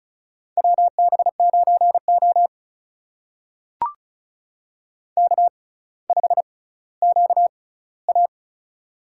Morse code, heard as a keyed tone, about 36 wpm, spaced as for 18 wpm, 700 Hz. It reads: WB9O K5QA
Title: USA Call Signs - 4 Characters - 6x Character Spacing - 35wpm